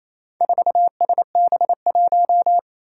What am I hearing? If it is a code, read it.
Morse code, heard as a keyed tone, about 29 wpm, 700 Hz. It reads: 4SB1
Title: Four Challenging Letter-Number Combinations 28wpm